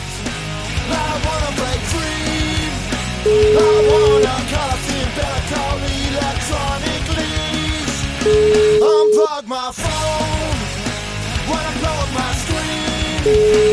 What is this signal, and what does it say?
Ringback tone (Europe, 425 Hz), with noise.